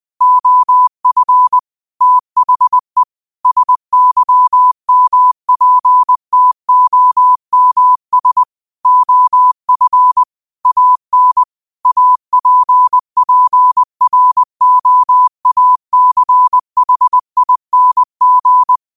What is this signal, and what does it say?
Morse code, heard as a keyed tone, about 20 wpm, 1000 Hz. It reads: OF THE SYMPTOMS OF AN APPROACHING
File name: qotdfiles / 20wpm